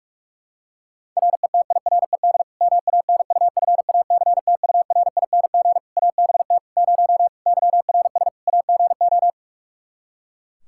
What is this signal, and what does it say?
Morse code, heard as a keyed tone, about 46 wpm, 700 Hz. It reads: RETIRED MANUFACTURING ABT 0 YRS AGO
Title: Practice QSO - Exchange 3 - More Info - 45wpm